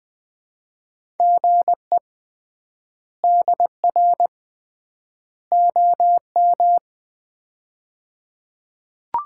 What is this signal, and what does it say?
Morse code, heard as a keyed tone, about 20 wpm, 700 Hz. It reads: GE DR OM
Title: Common QSO Phrases - 3x Word-Spacing - 20wpm